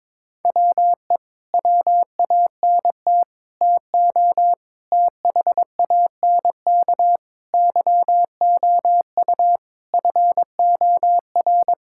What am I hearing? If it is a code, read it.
Morse code, heard as a keyed tone, about 21 wpm, 700 Hz. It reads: WE WANT TO THANK YOU FOR
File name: Sets of 9 Words - 22wpm